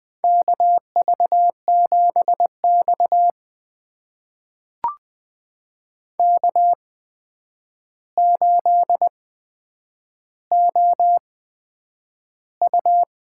Morse code, heard as a keyed tone, about 20 wpm, spaced as for 9 wpm, 700 Hz. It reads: KV7X K8OU
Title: USA Call Signs - 8x Character Spacing - 20wpm